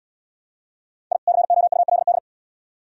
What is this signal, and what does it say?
Morse code, heard as a keyed tone, about 75 wpm, 700 Hz. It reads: I COULD